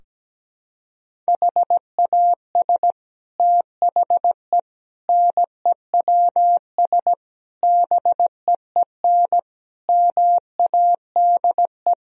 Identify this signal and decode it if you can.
Morse code, heard as a keyed tone, about 17 wpm, 700 Hz. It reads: HAS THE NEWS BEEN MADE
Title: Sentences from Top 1000 Words - Review Part 3 - 17wpm